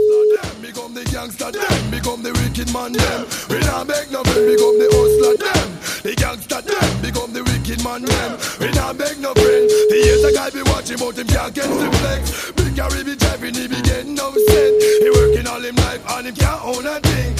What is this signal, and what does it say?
Ringback tone (Europe, 425 Hz), with noise.